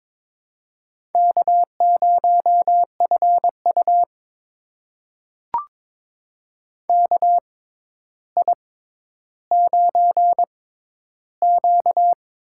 Morse code, heard as a keyed tone, about 22 wpm, spaced as for 11 wpm, 700 Hz. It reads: K0FU KI9Q
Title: USA Call Signs - 4 Characters - 6x Character Spacing - 22wpm